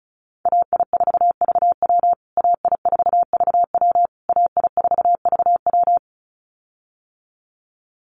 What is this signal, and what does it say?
Morse code, heard as a keyed tone, about 34 wpm, 700 Hz. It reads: AI4VW AI4VW AI4VW
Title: USA Call Signs Repeated 3x 35wpm